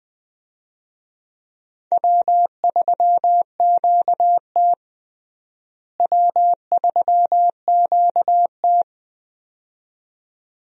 Morse code, heard as a keyed tone, about 21 wpm, 700 Hz. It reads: W3QT W3QT
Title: USA Call Signs - 4 Characters - Repeated 2x - 20wpm